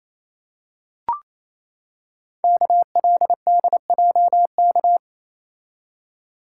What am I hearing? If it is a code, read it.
Morse code, heard as a keyed tone, about 29 wpm, 700 Hz. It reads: KLDJK
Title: Code Groups - 5 Letters - 28wpm